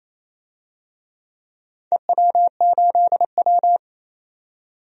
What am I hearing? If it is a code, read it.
Morse code, heard as a keyed tone, about 28 wpm, 700 Hz. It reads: EW8W
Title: DXCC-100 Call Signs - 4 Characters - 28wpm